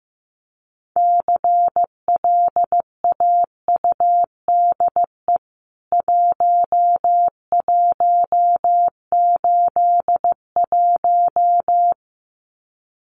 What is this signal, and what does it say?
Morse code, heard as a keyed tone, about 14 wpm, 700 Hz. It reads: CLAUDE 1181
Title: CWT Exchange 15wpm